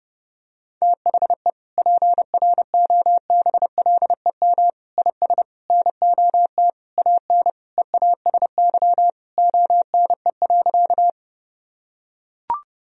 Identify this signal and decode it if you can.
Morse code, heard as a keyed tone, about 32 wpm, 700 Hz. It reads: THE PROBLEM IS NOT AN EASY ONE.
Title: Sentences from Top 1000 Words - Review Part 2 - 30wpm